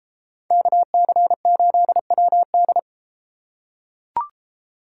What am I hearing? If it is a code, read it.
Morse code, heard as a keyed tone, about 33 wpm, 700 Hz. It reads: KC8WD